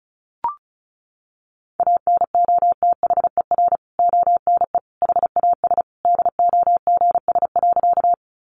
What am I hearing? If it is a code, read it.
Morse code, heard as a keyed tone, about 34 wpm, 700 Hz. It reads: ANOTHER ONE HAS DOGS.
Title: Additional Easy Phrases 35wpm